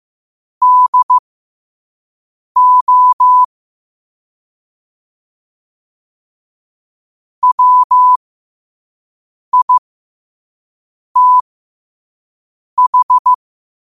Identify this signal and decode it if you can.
Morse code, heard as a keyed tone, about 15 wpm, spaced as for 5 wpm, 1000 Hz. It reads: DO WITH
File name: qotdfiles / 5wpm